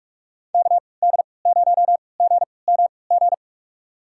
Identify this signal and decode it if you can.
Morse code, heard as a keyed tone, about 41 wpm, spaced as for 25 wpm, 700 Hz. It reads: KD0GMG
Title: USA Call Signs - 3x Character Spacing - 45wpm